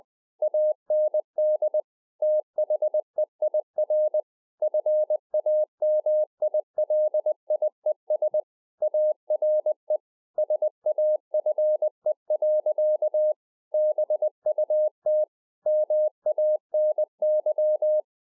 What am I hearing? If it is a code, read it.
Morse code, heard as a keyed tone, about 20 wpm, 610 Hz. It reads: AND THEIR FAMILIES ARE SAFE. BUT MANY